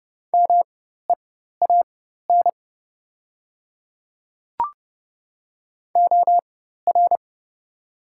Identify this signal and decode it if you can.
Morse code, heard as a keyed tone, about 30 wpm, spaced as for 14 wpm, 700 Hz. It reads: MEAN OR
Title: 4x Character Spacing - Top 200 Words - 30wpm